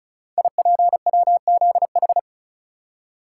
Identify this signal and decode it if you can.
Morse code, heard as a keyed tone, about 36 wpm, 690 Hz. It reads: IPWZH
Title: Code Groups - 5 Letters - 35wpm